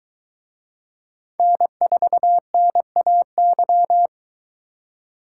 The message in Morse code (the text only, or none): N4NAY